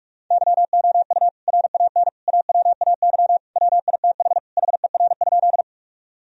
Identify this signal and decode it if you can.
Morse code, heard as a keyed tone, about 46 wpm, 700 Hz. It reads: YOU RAN AWAY WITH HER?
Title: Sentences from Top 1000 Words - Review Part 3 - 45wpm